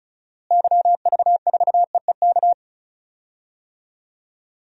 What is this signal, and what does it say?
Morse code, heard as a keyed tone, about 36 wpm, 700 Hz. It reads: YV4EEK
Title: DXCC-100 Call Signs - 6 Characters - Repeated 2x - 35wpm